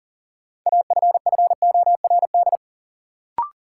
Morse code, heard as a keyed tone, about 40 wpm, 700 Hz. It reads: AFFORD
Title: Sentences from Top 1100 Words - New Word Drill Alt - 40wpm